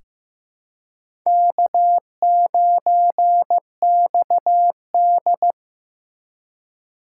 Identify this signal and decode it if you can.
Morse code, heard as a keyed tone, about 15 wpm, 700 Hz. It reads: K9XD
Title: K1USN SST QSO Practice - 15wpm